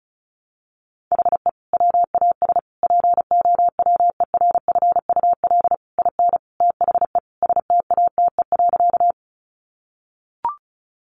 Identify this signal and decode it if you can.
Morse code, heard as a keyed tone, about 34 wpm, 700 Hz. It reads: HE WAS POWERFUL IN THE STATE.
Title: Sentences from Top 1300 Words 35wpm